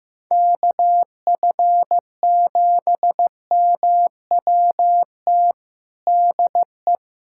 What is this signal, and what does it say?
Morse code, heard as a keyed tone, about 15 wpm, 700 Hz. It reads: KF7MWT DE
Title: USA Callsign DE Callsign - 6 Characters - 15wpm